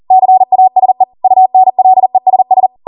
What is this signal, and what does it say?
Morse code, heard as a keyed tone, about 40 wpm, 750 Hz. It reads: CASE UNLESS